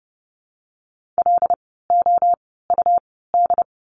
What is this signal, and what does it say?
Morse code, heard as a keyed tone, about 30 wpm, spaced as for 17 wpm, 700 Hz. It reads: LOUD
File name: Lesson 17f - Letter u - Words 30wpm